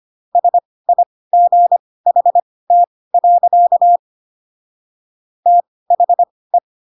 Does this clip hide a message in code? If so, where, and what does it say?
Yes: Morse code from 0.4 to 6.6 s, heard as a keyed tone, about 25 wpm, spaced as for 18 wpm, 700 Hz. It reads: SIGHT. THE